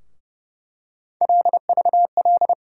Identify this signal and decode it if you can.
Morse code, heard as a keyed tone, about 32 wpm, 700 Hz. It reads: LVL
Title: USA Call Sign Postfixes - 3 Letters 30wpm